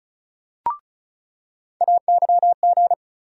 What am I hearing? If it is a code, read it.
Morse code, heard as a keyed tone, about 36 wpm, 700 Hz. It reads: AYG
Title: USA Call Sign Postfixes - 3 Letters 35wpm